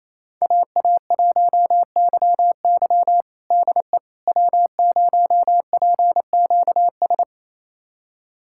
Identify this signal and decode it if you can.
Morse code, heard as a keyed tone, about 28 wpm, 700 Hz. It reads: AA1YY DE W0PQS
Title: USA Callsign DE Callsign - 5 Characters - 28wpm